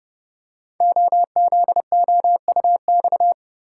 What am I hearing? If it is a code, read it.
Morse code, heard as a keyed tone, about 30 wpm, 700 Hz. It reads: OZOUX
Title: Code Groups - 5 Letter:Numbers - 30wpm